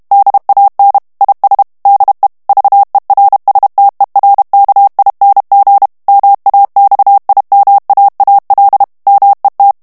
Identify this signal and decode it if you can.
Morse code, heard as a keyed tone, about 32 wpm, 800 Hz. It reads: DAN IS DE VERSTERKING MAXIMAAL MET
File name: morse mp3 / HOLLANDAIS texte clair 1 vitesse 30 mots minute